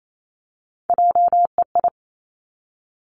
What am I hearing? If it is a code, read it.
Morse code, heard as a keyed tone, about 28 wpm, 690 Hz. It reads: JEI
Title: Three Challenging Letter-Number Combinations 28wpm